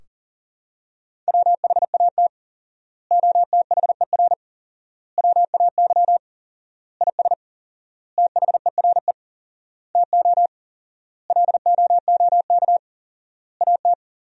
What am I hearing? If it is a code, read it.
Morse code, heard as a keyed tone, about 37 wpm, 700 Hz. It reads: WHAT OTHER WAY IS THERE TO LOOK AT
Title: 4x Word-Spacing - Sentences from Top 100 Words - 40wpm